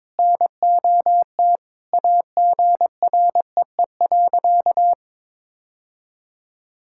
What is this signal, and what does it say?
Morse code, heard as a keyed tone, about 22 wpm, 700 Hz. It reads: NOT AGREE.
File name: practice / 22wpm / Sentences with 3-Word Phrases - 22wpm